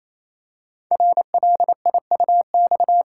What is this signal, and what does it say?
Morse code, heard as a keyed tone, about 27 wpm, 700 Hz. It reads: RLIUX
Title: Code Groups - 5 Letter:Numbers - 28wpm